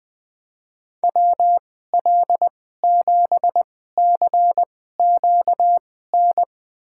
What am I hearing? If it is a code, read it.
Morse code, heard as a keyed tone, about 20 wpm, spaced as for 15 wpm, 700 Hz. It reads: WL7CQN
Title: USA Call Signs - 6 Characters - 2x Character Spacing - 20wpm